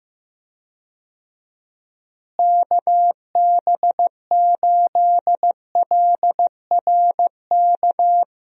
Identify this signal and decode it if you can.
Morse code, heard as a keyed tone, about 15 wpm, 700 Hz. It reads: KB8LRK
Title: USA Call Signs - 6 Characters - Repeated 3x - 15wpm